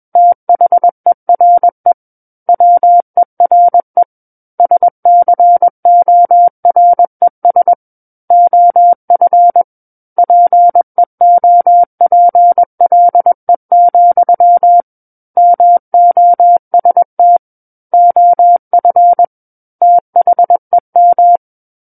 Morse code, heard as a keyed tone, about 21 wpm, 700 Hz. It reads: THERE WERE SCORES OF PEOPLE, MOST OF THEM
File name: Book One / 12-Chapter 12 - 21 WPM